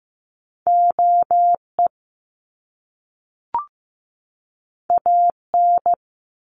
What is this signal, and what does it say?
Morse code, heard as a keyed tone, about 15 wpm, 700 Hz. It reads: OE AN
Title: Top Two Letter Combinations in Words 15wpm